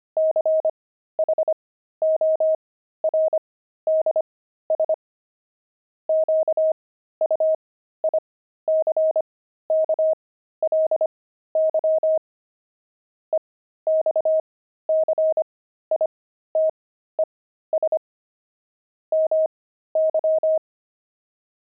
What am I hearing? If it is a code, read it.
Morse code, heard as a keyed tone, about 25 wpm, spaced as for 13 wpm, 630 Hz. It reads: CHORDS QUICKLY EXCITES MY
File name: pangram-fs-13-cs-25-jazz